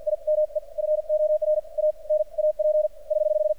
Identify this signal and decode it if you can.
Morse code, heard as a keyed tone, about 46 wpm, spaced as for 41 wpm, 610 Hz. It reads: IME FOR A NAP <HH>
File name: archived / CRSnet